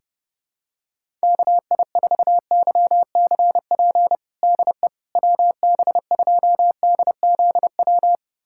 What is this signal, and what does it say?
Morse code, heard as a keyed tone, about 32 wpm, 700 Hz. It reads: KI4YCP DE WB2DZW
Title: USA Callsign DE Callsign - 6 Characters - 30wpm